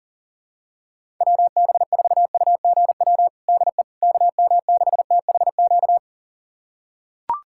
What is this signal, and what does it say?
Morse code, heard as a keyed tone, about 40 wpm, 700 Hz. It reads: WB4UGW DE KM6THQ
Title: USA Callsign DE Callsign - 6 Characters - 40wpm